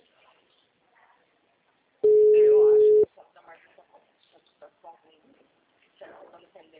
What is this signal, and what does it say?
Ringback tone (Europe, 425 Hz), with speech.